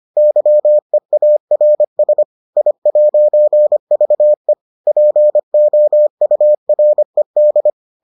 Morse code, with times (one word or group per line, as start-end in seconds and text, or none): 0.2-2.2 YEARS
2.6-4.5 I'VE
4.9-7.7 POURED